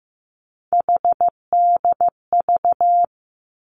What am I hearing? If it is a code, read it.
Morse code, heard as a keyed tone, about 15 wpm, 700 Hz. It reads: HDV
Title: Three Challenging Letter-Number Combinations 15wpm